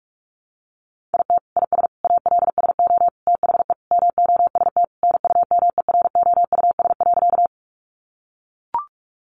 Morse code, heard as a keyed tone, about 44 wpm, 700 Hz. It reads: IT IS ALSO THE MOST NUMEROUS.
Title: Sentences from Top 1400 Words 45wpm